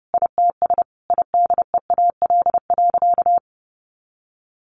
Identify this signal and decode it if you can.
Morse code, heard as a keyed tone, about 30 wpm, 700 Hz. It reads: ITS IDEAL.
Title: Sentences from Top 1400 Words 30wpm